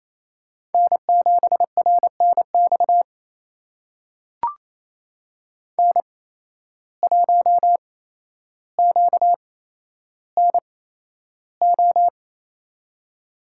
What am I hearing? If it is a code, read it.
Morse code, heard as a keyed tone, about 28 wpm, spaced as for 12 wpm, 700 Hz. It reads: N7RNX N1QNO
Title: USA Call Signs - 5 Characters - 8x Character Spacing - 28wpm